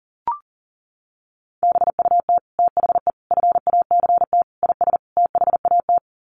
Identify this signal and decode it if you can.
Morse code, heard as a keyed tone, about 40 wpm, 700 Hz. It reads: BUT THE FACT IS THAT
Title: Sets of 5 Words Encore2 40wpm